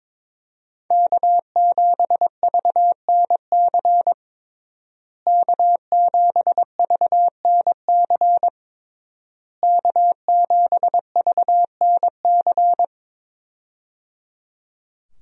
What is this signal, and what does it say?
Morse code, heard as a keyed tone, about 22 wpm, 700 Hz. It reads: K7VNC K7VNC K7VNC
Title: USA Call Signs - 5 Characters - Repeated 3x - 22wpm